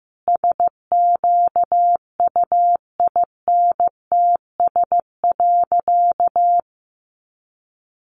Morse code, heard as a keyed tone, about 15 wpm, 700 Hz. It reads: SQUINTS.
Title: Additional Easy Phrases 15wpm